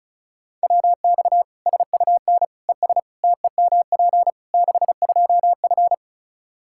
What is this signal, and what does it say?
Morse code, heard as a keyed tone, about 36 wpm, 700 Hz. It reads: WX SUN ES TEMP 62F
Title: Practice QSO - Exchange 2 - Info - 35wpm